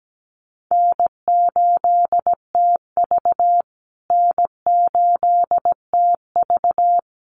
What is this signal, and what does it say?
Morse code, heard as a keyed tone, about 17 wpm, 700 Hz. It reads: N8TV N8TV
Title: USA Call Signs Repeated 3x 17wpm